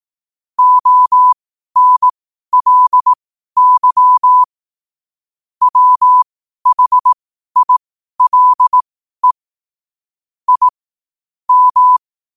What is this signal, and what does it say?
Morse code, heard as a keyed tone, about 18 wpm, spaced as for 12 wpm, 1000 Hz. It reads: ONLY WHILE I M
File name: qotdfiles / 12wpm